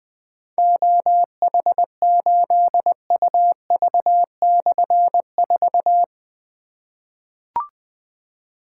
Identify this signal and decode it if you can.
Morse code, heard as a keyed tone, about 20 wpm, 700 Hz. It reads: OH8UV/4
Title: Difficult Call Signs 20wpm